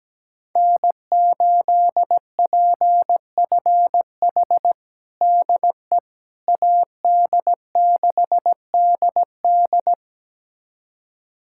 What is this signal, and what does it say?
Morse code, heard as a keyed tone, about 17 wpm, 700 Hz. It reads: N8PFH DE AD6DD
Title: USA Callsign DE Callsign - 5 Characters - 17wpm